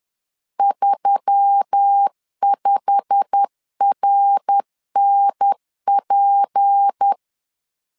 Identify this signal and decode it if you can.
Morse code, heard as a keyed tone, about 10 wpm, 800 Hz. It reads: 35RNP
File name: CW / DET / 1 9 1500C S A 10wpm 4511